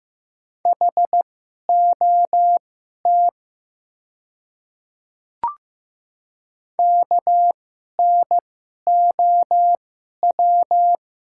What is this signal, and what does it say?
Morse code, heard as a keyed tone, about 15 wpm, spaced as for 11 wpm, 700 Hz. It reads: HOT KNOW
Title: 2x Character Spacing - Top 100 Words - 15wpm